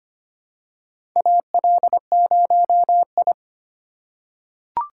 Morse code, heard as a keyed tone, about 25 wpm, 700 Hz. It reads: AL0I
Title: USA Call Signs - 4 Characters - 2x Character Spacing - 25wpm